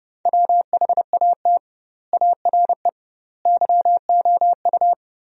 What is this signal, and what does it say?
Morse code, heard as a keyed tone, about 29 wpm, 700 Hz. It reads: WHAT ARE YOU
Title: 2x Word-Spacing - Sentences from Top 100 Words - 30wpm